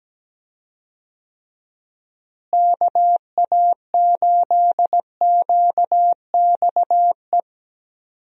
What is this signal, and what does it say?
Morse code, heard as a keyed tone, about 17 wpm, 700 Hz. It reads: KA8QXE
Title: USA Call Signs - 6 Characters - Repeated 3x - 17wpm